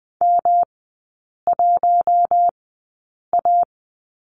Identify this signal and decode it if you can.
Morse code, heard as a keyed tone, about 20 wpm, spaced as for 8 wpm, 700 Hz. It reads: M1A
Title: 107 Lesson 18e - Letter m - Three Characters 20wpm